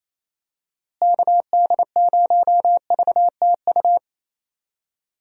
Morse code, heard as a keyed tone, about 27 wpm, 700 Hz. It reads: KD0VTU
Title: USA Call Signs - 6 Characters - Repeated 3x - 28wpm